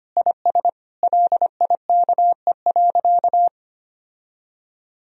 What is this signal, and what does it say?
Morse code, heard as a keyed tone, about 25 wpm, 700 Hz. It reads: IS LIKE.
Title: Sentences from Top 1000 Words - Review Part 3 - 25wpm